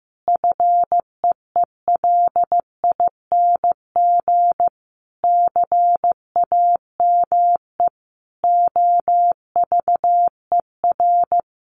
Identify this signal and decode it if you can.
Morse code, heard as a keyed tone, about 14 wpm, 700 Hz. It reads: FEELING CAME OVER
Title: Sentences from Top 1200 Words 15wpm